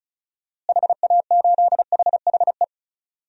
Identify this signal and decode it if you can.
Morse code, heard as a keyed tone, about 36 wpm, 700 Hz. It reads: HA8HHE
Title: DXCC-100 Call Signs - 6 Characters - Repeated 2x - 35wpm